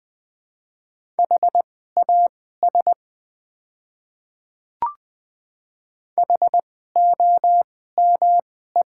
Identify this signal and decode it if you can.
Morse code, heard as a keyed tone, about 20 wpm, spaced as for 15 wpm, 690 Hz. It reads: HAS HOME
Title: 2x Character Spacing - Top 200 Words - 20wpm